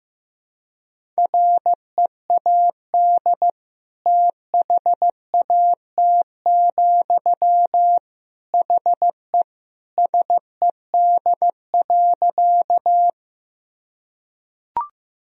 Morse code, heard as a keyed tone, about 15 wpm, 700 Hz. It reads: READ THAT, HE SED.
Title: Sentences from Top 200 Words and Abbreviations 15wpm